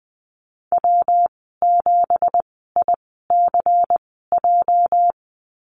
Morse code, heard as a keyed tone, about 20 wpm, spaced as for 15 wpm, 700 Hz. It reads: W7ICJ
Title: 213 Lesson 34g - Number 8 - Call Signs 20wpm